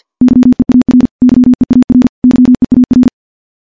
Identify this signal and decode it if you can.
Morse code, heard as a keyed tone, about 51 wpm, 250 Hz. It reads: YEAR YEAR YEAR
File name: morse / morse-training-50WPM